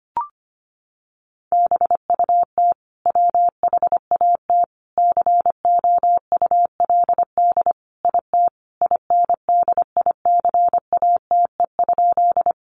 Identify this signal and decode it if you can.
Morse code, heard as a keyed tone, about 25 wpm, 700 Hz. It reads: BUT WHAT COULD IT INDICATE?
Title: Sentences from Top 1300 Words 25wpm